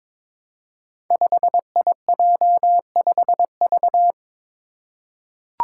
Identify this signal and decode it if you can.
Morse code, heard as a keyed tone, about 22 wpm, 700 Hz. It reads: 5IJ5V
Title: Five Challenging Letter-Number Combinations 22wpm